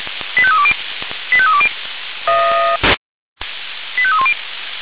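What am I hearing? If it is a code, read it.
Five-tone selective call (ZVEI), heard as quick runs of five tones, tones of about 70 ms. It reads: 85320 85320 85310
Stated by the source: ZVEI five-tone call 85894 received with an SDR.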